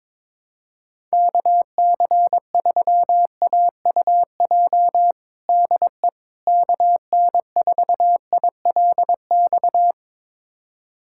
Morse code, heard as a keyed tone, about 22 wpm, 700 Hz. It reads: KC3AUJ DE KN4ILX
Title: USA Callsign DE Callsign - 6 Characters - 22wpm